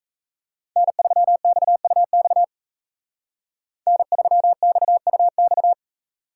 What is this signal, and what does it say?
Morse code, heard as a keyed tone, about 33 wpm, 700 Hz. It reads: N3XUX N3XUX
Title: Speed Racing - USA 1x3 Call Signs 28wpm